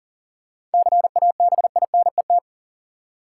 Morse code, heard as a keyed tone, about 40 wpm, 700 Hz. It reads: CABINET
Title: Seven Letter Words - 40wpm